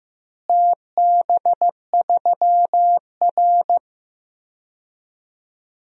Morse code, heard as a keyed tone, about 15 wpm, 700 Hz. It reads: TB3R
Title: DXCC-100 Call Signs - 4 Characters - Repeated 3x - 15wpm